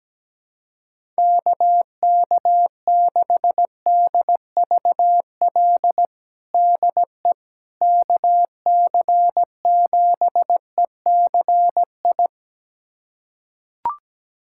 Morse code, heard as a keyed tone, about 17 wpm, 700 Hz. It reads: KK6DVL DE KC7ECI
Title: USA Callsign DE Callsign - 6 Characters - 17wpm